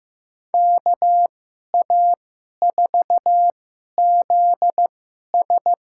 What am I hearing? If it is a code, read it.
Morse code, heard as a keyed tone, about 15 wpm, spaced as for 11 wpm, 700 Hz. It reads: KA4ZS
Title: USA Call Signs - 2x Character Spacing - 15wpm